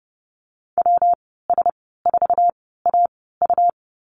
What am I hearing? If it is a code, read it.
Morse code, heard as a keyed tone, about 30 wpm, 700 Hz. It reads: W S 4 A U
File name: Lesson 19g - Letter w - Call Signs 30wpm